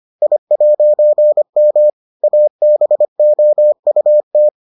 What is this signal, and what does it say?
Morse code, heard as a keyed tone, about 25 wpm, 600 Hz. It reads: I'M ABOUT